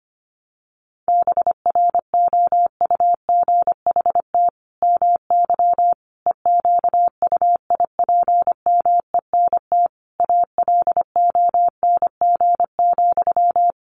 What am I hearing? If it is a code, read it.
Morse code, heard as a keyed tone, about 25 wpm, 700 Hz. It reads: BROUGHT MY EQUIPMENT ALONG,
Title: Sentences from Top 700 Words 25wpm